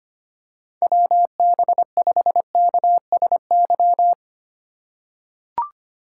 Morse code, heard as a keyed tone, about 25 wpm, 700 Hz. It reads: WB5KSY
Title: USA Call Signs - 6 Characters - 2x Character Spacing - 25wpm